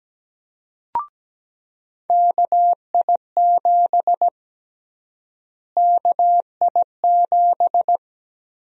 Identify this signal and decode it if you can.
Morse code, heard as a keyed tone, about 17 wpm, 700 Hz. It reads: KI7 KI7
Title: USA Call Sign Prefixes - Repeated 2x - 17wpm